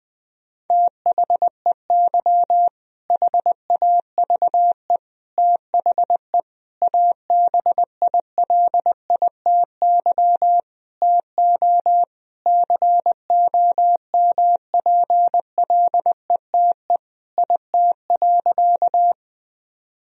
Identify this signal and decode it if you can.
Morse code, heard as a keyed tone, about 20 wpm, 700 Hz. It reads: THEY HAVE THE ABILITY TO COMPLETE IT.
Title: Sentences from Top 1000 Words - Review Part 3 - 20wpm